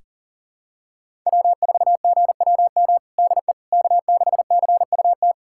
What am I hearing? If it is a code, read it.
Morse code, heard as a keyed tone, about 40 wpm, 700 Hz. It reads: W4GWM DE K6CUT
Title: USA Callsign DE Callsign - 5 Characters - 40wpm